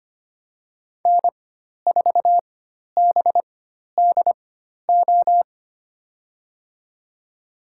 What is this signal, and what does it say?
Morse code, heard as a keyed tone, about 25 wpm, spaced as for 12 wpm, 700 Hz. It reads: N4BDO
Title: USA Call Signs - 5 Characters - 4x Character Spacing - 25wpm